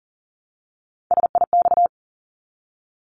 Morse code, heard as a keyed tone, about 40 wpm, 700 Hz. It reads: SIX
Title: Top 500 Words Repeated 40wpm